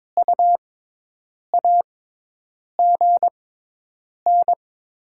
Morse code, heard as a keyed tone, about 22 wpm, 700 Hz. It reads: U A G N
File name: Top 20 QSO Elements - 6x Character Spacing - 22wpm